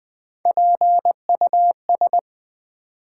Morse code, heard as a keyed tone, about 20 wpm, 700 Hz. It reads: PUS
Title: Build A Word - 1 Letter At A Time - 4-Letter Words - 20wpm